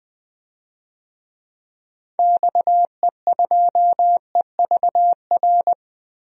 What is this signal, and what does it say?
Morse code, heard as a keyed tone, about 20 wpm, 700 Hz. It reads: XE2EVR
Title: DXCC-100 Call Signs - 6 Characters - 20wpm